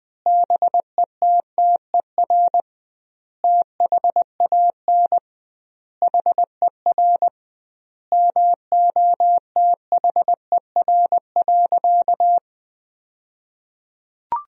Morse code, heard as a keyed tone, about 20 wpm, 700 Hz. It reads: BETTER THAN HER MOTHER.
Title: 2x Word-Spacing - Sentences from Top 400 Words - 20wpm